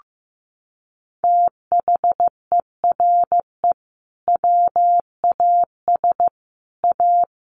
Morse code, heard as a keyed tone, about 15 wpm, 700 Hz. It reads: THERE WAS A
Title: Sentences from Top 1300 Words 15wpm